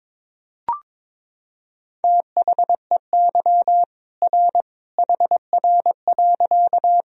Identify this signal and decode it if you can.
Morse code, heard as a keyed tone, about 22 wpm, 700 Hz. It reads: THEY R HR.
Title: Sentences from Top 200 Words and Abbreviations 22wpm